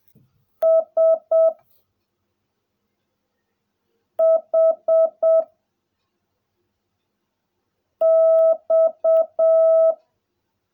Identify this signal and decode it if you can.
Morse code, heard as a keyed tone, about 7 wpm, 660 Hz. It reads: S H X
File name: jh8cbh / 2021cw / cw16a